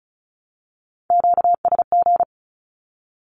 Morse code, heard as a keyed tone, about 36 wpm, 700 Hz. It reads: QSG
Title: Q-Codes with Spoken Meaning - 35wpm